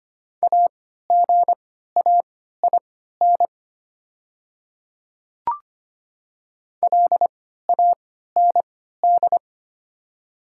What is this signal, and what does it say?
Morse code, heard as a keyed tone, about 25 wpm, spaced as for 14 wpm, 700 Hz. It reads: AGAIN LAND
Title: 3x Character Spacing - Top 200 Words - 25wpm